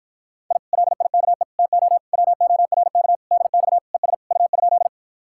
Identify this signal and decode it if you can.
Morse code, heard as a keyed tone, about 52 wpm, 700 Hz. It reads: I LIKE TO WORK DX ES U?